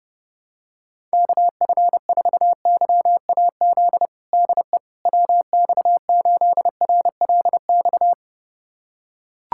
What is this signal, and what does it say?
Morse code, heard as a keyed tone, about 30 wpm, 700 Hz. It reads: KF4YAZ DE WX8RLX
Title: USA Callsign DE Callsign 30wpm